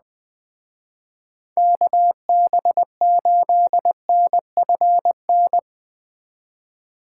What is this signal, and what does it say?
Morse code, heard as a keyed tone, about 20 wpm, 700 Hz. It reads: KB8NFN
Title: Speed Racing - USA 2x3 Call Signs 17wpm